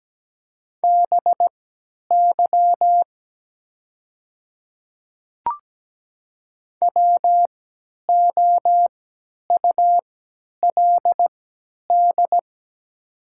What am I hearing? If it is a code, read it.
Morse code, heard as a keyed tone, about 17 wpm, spaced as for 10 wpm, 700 Hz. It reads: BY WOULD
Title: 3x Character Spacing - Top 100 Words - 17wpm